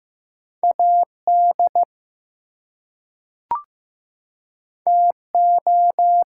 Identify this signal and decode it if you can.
Morse code, heard as a keyed tone, about 15 wpm, 700 Hz. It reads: AD TO
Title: Two Letter Words - 15wpm